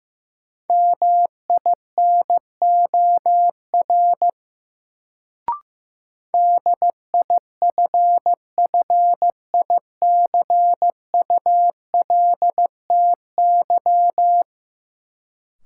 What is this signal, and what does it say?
Morse code, heard as a keyed tone, about 16 wpm, 700 Hz. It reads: MINOR DIFFICULTY
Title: Sentences from Top 1500 Words - New Word Drill - 15wpm